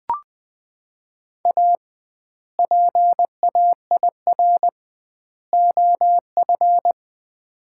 Morse code, heard as a keyed tone, about 20 wpm, 700 Hz. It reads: A PAIR OF
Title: 2x Word-Spacing - Sets of 3 Words - 20wpm